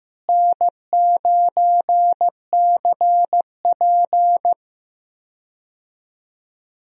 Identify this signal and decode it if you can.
Morse code, heard as a keyed tone, about 15 wpm, 700 Hz. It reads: N9CP
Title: USA Call Signs - 4 Characters - Repeated 2x - 15wpm